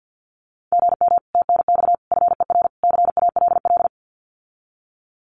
Morse code, heard as a keyed tone, about 50 wpm, 700 Hz. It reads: GM TNX FER CALL
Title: Common QSO Phrases Encore 50wpm